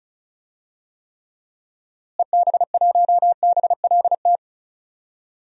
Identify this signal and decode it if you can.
Morse code, heard as a keyed tone, about 36 wpm, 700 Hz. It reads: EB1BLT
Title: DXCC-100 Call Signs - 6 Characters - Repeated 3x - 35wpm